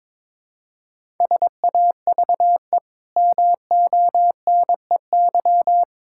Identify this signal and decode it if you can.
Morse code, heard as a keyed tone, about 22 wpm, 700 Hz. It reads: SAVE MONEY
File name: practice / 22wpm / Easy Sentences with Hints - 22wpm